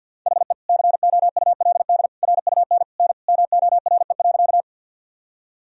Morse code, heard as a keyed tone, about 50 wpm, 700 Hz. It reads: HE COULD RUN N MORE.
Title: Sentences from Top 300 Words and Abbreviations 50wpm